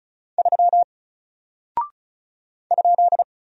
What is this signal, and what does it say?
Morse code, heard as a keyed tone, about 36 wpm, 700 Hz. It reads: 3 ?
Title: Single Character (Letter:Number:Punctuation:Prosign) Alt 35wpm